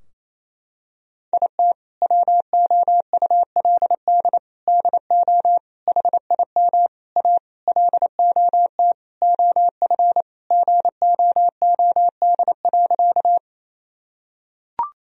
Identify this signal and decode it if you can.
Morse code, heard as a keyed tone, about 28 wpm, 700 Hz. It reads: IT WOULD DO HIM A LOT OF GOOD.
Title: Sentences from Top 1000 Words - Review Part 3 - 28wpm